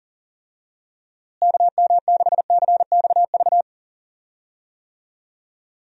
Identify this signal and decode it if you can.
Morse code, heard as a keyed tone, about 40 wpm, 700 Hz. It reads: KM6CXV
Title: USA Call Signs - 6 Characters - Repeated 3x - 40wpm